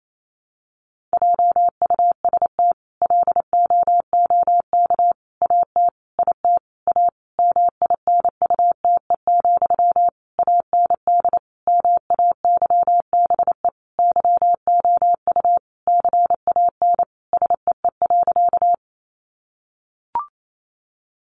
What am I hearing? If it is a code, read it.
Morse code, heard as a keyed tone, about 28 wpm, 700 Hz. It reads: JUST LOOK AT IT A MINUTE, AND MAYBE YOU CAN SEE.
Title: Sentences from Top 1000 Words 28wpm